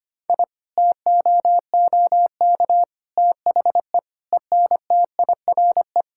Morse code, heard as a keyed tone, about 25 wpm, 700 Hz. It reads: I TOOK THE ENTIRE
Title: Sentences from Top 1000 Words - Review Part 5 - 25wpm